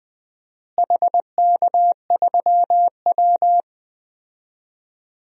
Morse code, heard as a keyed tone, about 20 wpm, 700 Hz. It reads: HK3W
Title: DXCC-100 Call Signs - 4 Characters - Repeated 3x - 20wpm